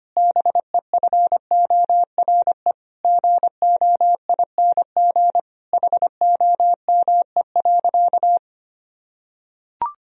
Morse code, heard as a keyed tone, about 25 wpm, 700 Hz. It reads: BEFORE GOING HOME.
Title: Sentences from Top 1000 Words - Review Part 3 - 25wpm